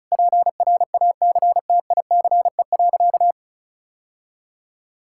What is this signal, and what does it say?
Morse code, heard as a keyed tone, about 34 wpm, 700 Hz. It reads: PRACTICE.
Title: Sentences from Top 1000 Words - Review Part 5 - 35wpm